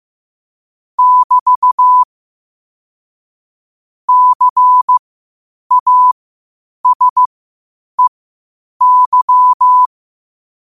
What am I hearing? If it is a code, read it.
Morse code, heard as a keyed tone, about 15 wpm, spaced as for 8 wpm, 1000 Hz. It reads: <BT> CASEY
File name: qotdfiles / 8wpm